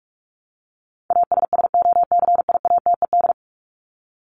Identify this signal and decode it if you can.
Morse code, heard as a keyed tone, about 44 wpm, 700 Hz. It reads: ASSOCIATED
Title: ED Suffix - Top 100 Words with Base Words 45wpm